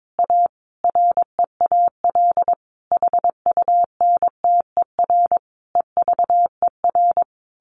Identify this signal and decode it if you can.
Morse code, heard as a keyed tone, about 22 wpm, 700 Hz. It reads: A REAL HUNTER EVER
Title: A Sound Of Thunder 22wpm